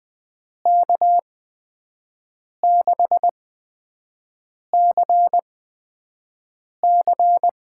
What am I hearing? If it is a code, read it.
Morse code, heard as a keyed tone, about 20 wpm, spaced as for 5 wpm, 700 Hz. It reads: K6CC
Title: USA Call Signs - 6 Characters - 8x Character Spacing - 20wpm